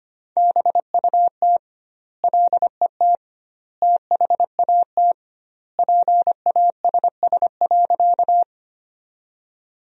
Morse code, heard as a keyed tone, about 25 wpm, 700 Hz. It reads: BUT LET THAT PASS.
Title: 2x Word-Spacing - Sentences from Top 400 Words - 25wpm